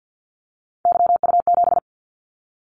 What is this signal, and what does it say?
Morse code, heard as a keyed tone, about 50 wpm, 700 Hz. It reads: YU7
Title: Callsign Prefixes for Top 25 DXCC - 50wpm